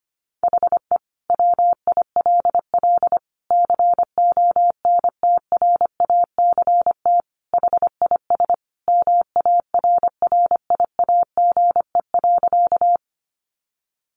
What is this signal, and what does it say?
Morse code, heard as a keyed tone, about 25 wpm, 700 Hz. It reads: HE WILL CONTRACT HIS MARRIAGE.
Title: Sentences from Top 1400 Words 25wpm